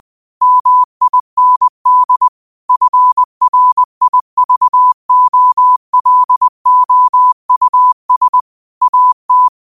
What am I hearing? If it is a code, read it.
Morse code, heard as a keyed tone, about 20 wpm, 1000 Hz. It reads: MIND FRIVOLOUS AT